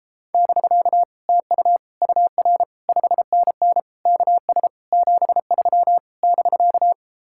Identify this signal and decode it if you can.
Morse code, heard as a keyed tone, about 36 wpm, 700 Hz. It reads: <BK> TU UR 5NN KS 73 <BK>